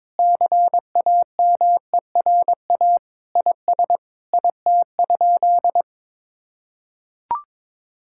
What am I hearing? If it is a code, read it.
Morse code, heard as a keyed tone, about 21 wpm, 700 Hz. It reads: CAMERA IS IT?
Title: Sentences from Top 1000 Words - Review Part 5 - 22wpm